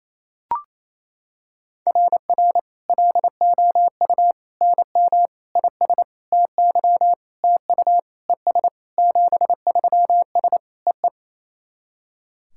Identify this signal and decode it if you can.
Morse code, heard as a keyed tone, about 28 wpm, 700 Hz. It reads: RR LOU NM IS TY TU ES 73S EE
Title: SOTA QSO Practice 28wpm